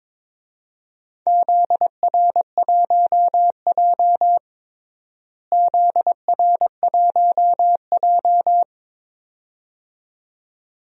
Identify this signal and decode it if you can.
Morse code, heard as a keyed tone, about 22 wpm, 700 Hz. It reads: ZR1J ZR1J
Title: DXCC-100 Call Signs - 4 Characters - Repeated 2x - 22wpm